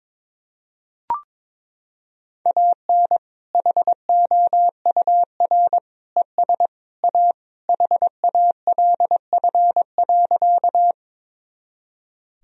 Morse code, heard as a keyed tone, about 22 wpm, 700 Hz. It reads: AN HOUR ES A HALF.
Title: Sentences from Top 400 Words and Abbreviations 22wpm